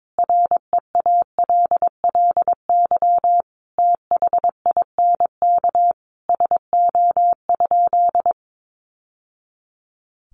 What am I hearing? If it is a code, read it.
Morse code, heard as a keyed tone, about 22 wpm, 700 Hz. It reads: REALLY THINK SO?
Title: Sentences from Top 600 Words 22wpm